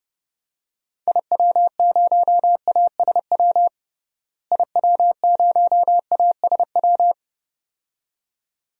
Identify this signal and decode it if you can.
Morse code, heard as a keyed tone, about 30 wpm, 700 Hz. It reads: IW0ASW IW0ASW
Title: DXCC-100 Call Signs - 6 Characters - Repeated 2x - 30wpm